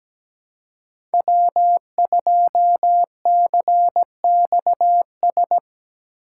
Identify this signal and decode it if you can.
Morse code, heard as a keyed tone, about 17 wpm, 700 Hz. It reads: W2CXS
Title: Code Groups - 5 Letter:Numbers - 17wpm